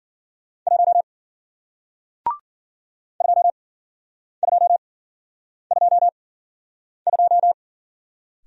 Morse code, heard as a keyed tone, about 52 wpm, spaced as for 9 wpm, 700 Hz. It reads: 1 2222